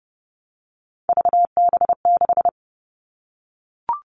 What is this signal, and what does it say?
Morse code, heard as a keyed tone, about 32 wpm, 700 Hz. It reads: VB6
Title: Three Challenging Letter-Number Combinations 30wpm